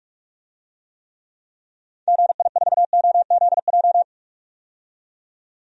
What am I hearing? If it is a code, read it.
Morse code, heard as a keyed tone, about 42 wpm, 700 Hz. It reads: GI4OZJ
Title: DXCC-100 Call Signs - 6 Characters - 45wpm